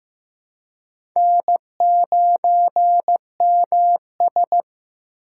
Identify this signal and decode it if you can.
Morse code, heard as a keyed tone, about 15 wpm, 700 Hz. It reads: N9MS
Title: K1USN SST QSO Practice - 15wpm